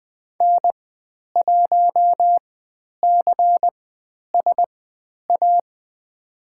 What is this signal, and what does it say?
Morse code, heard as a keyed tone, about 20 wpm, spaced as for 10 wpm, 700 Hz. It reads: N1CSA
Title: 109 Lesson 18g - Letter m - Call Signs 20wpm